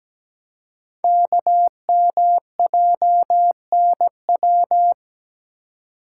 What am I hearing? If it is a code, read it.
Morse code, heard as a keyed tone, about 17 wpm, 700 Hz. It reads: KMJNW
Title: Code Groups - 5 Letters - 17wpm